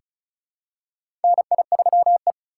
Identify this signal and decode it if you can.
Morse code, heard as a keyed tone, about 36 wpm, 700 Hz. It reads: NI3E
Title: USA Call Signs - 4 Characters - 3x Character Spacing - 35wpm